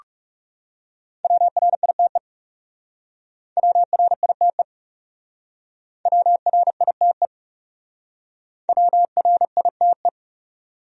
Morse code, heard as a keyed tone, about 39 wpm, 700 Hz. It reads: WRITE WRITE WRITE WRITE